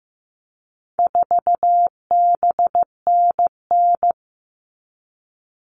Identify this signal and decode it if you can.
Morse code, heard as a keyed tone, about 15 wpm, 700 Hz. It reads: 4BNN
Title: All Cut Numbers - 4-Digit 15wpm